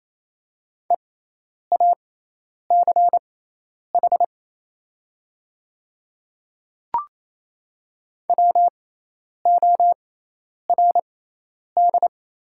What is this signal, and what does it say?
Morse code, heard as a keyed tone, about 28 wpm, spaced as for 10 wpm, 700 Hz. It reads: EACH WORD
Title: 6x Character Spacing - Top 200 Words - 28wpm